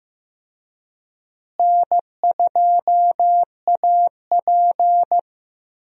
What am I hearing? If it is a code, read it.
Morse code, heard as a keyed tone, about 15 wpm, 700 Hz. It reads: N2AP
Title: USA Call Signs - 4 Characters - Repeated 3x - 15wpm